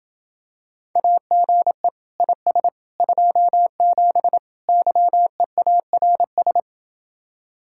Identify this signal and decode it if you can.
Morse code, heard as a keyed tone, about 28 wpm, 700 Hz. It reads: AGE IS 27 YEARS